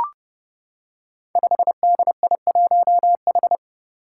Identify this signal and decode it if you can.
Morse code, heard as a keyed tone, about 32 wpm, 700 Hz. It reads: 5DI1H
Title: Five Challenging Letter-Number Combinations 30wpm